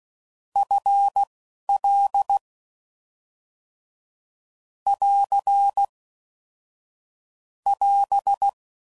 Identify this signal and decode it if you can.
Morse code, heard as a keyed tone, about 16 wpm, spaced as for 11 wpm, 800 Hz. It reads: FL <AR> <AS>
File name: Mixed / Short Runs / ShortContrasts